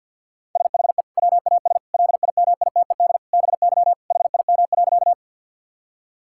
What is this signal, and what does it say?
Morse code, heard as a keyed tone, about 50 wpm, 700 Hz. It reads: SHE WAS LIMITED BY HIM.
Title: Sentences from Top 1000 Words - Review Part 6 - 50wpm